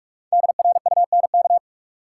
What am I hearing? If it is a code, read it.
Morse code, heard as a keyed tone, about 47 wpm, 700 Hz. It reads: DRUNK
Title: Five Letter Words - 45wpm